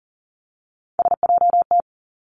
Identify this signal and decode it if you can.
Morse code, heard as a keyed tone, about 40 wpm, 700 Hz. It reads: SJT
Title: Three Challenging Letter-Number Combinations 40wpm